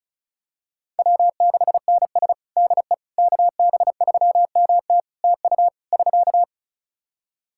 Morse code, heard as a keyed tone, about 35 wpm, 700 Hz. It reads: W6NS DE KB3MT TU <SK>
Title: Practice QSO - Exchange 4 - Ending - 35wpm